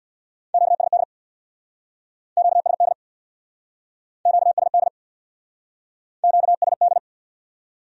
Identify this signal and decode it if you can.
Morse code, heard as a keyed tone, about 68 wpm, 700 Hz. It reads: QSD QSD QSD QSD